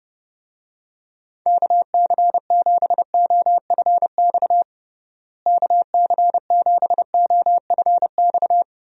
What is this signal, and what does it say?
Morse code, heard as a keyed tone, about 30 wpm, 700 Hz. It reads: KC7OFX KC7OFX
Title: USA Call Signs - 6 Characters - Repeated 3x - 30wpm